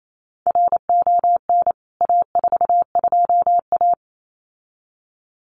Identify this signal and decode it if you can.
Morse code, heard as a keyed tone, about 28 wpm, 700 Hz. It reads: RON A42A
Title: CWT Exchange 28wpm